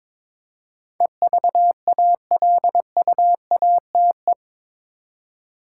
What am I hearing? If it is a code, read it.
Morse code, heard as a keyed tone, about 22 wpm, 700 Hz. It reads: EVALUATE
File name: practice / 22wpm / Eight Letter Words - 22wpm